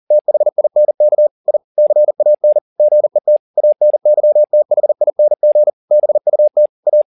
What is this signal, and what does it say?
Morse code, heard as a keyed tone, about 40 wpm, 600 Hz. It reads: THINK I CAN GET ANYTHING BUT A